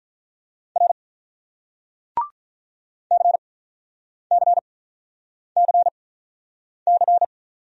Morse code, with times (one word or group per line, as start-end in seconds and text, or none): 0.8-0.9 R
3.1-7.2 CCCC